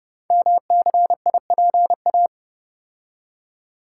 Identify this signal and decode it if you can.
Morse code, heard as a keyed tone, about 30 wpm, 700 Hz. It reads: MCIPA
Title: Code Groups - 5 Letters - 30wpm